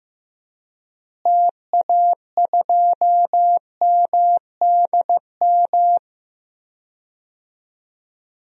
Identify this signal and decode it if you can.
Morse code, heard as a keyed tone, about 15 wpm, 700 Hz. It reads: TA2MDM
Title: DXCC-100 Call Signs - 6 Characters - Repeated 2x - 15wpm